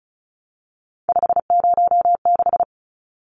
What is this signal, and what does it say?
Morse code, heard as a keyed tone, about 34 wpm, 700 Hz. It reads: 506
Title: Numbers - 3-Digit 35wpm